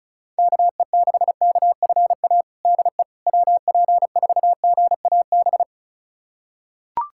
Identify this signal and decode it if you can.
Morse code, heard as a keyed tone, about 36 wpm, 700 Hz. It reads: KE6KFA DE WP4GAB
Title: USA Callsign DE Callsign - 6 Characters - 35wpm